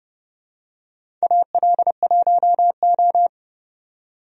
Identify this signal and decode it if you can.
Morse code, heard as a keyed tone, about 30 wpm, 700 Hz. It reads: AL1O
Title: USA Call Signs - 4 Characters - 30wpm